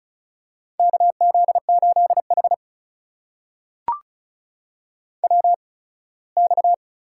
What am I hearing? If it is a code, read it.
Morse code, heard as a keyed tone, about 36 wpm, spaced as for 21 wpm, 700 Hz. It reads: KZ8H WX
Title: USA Call Signs - 8x Character Spacing - 35wpm